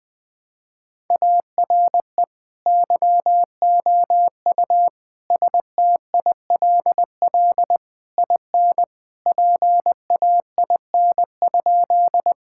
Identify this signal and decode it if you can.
Morse code, heard as a keyed tone, about 20 wpm, 700 Hz. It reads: ARE YOU STILL IN PAIN?
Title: Sentences from Top 1600 Words 20wpm